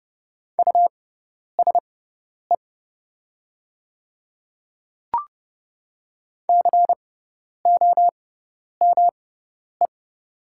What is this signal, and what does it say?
Morse code, heard as a keyed tone, about 30 wpm, spaced as for 10 wpm, 700 Hz. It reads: USE COME
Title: 6x Character Spacing - Top 100 Words - 30wpm